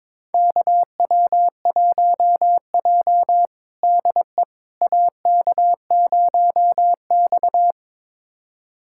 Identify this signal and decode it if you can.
Morse code, heard as a keyed tone, about 21 wpm, 700 Hz. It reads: KW1J DE AK0X
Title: USA Callsign DE Callsign - 4 Characters - 22wpm